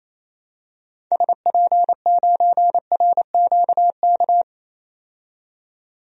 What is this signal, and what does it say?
Morse code, heard as a keyed tone, about 28 wpm, 700 Hz. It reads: SP9RQK
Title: DXCC-100 Call Signs - 6 Characters - 28wpm